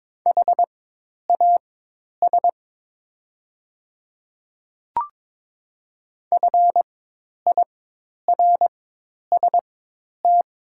Morse code, heard as a keyed tone, about 22 wpm, spaced as for 10 wpm, 700 Hz. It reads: HAS FIRST
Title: 4x Character Spacing - Top 100 Words - 22wpm